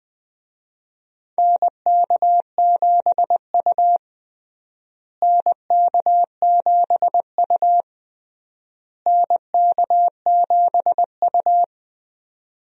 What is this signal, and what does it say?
Morse code, heard as a keyed tone, about 20 wpm, 700 Hz. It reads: NK7U NK7U NK7U
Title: USA Call Signs - 4 Characters - Repeated 3x - 20wpm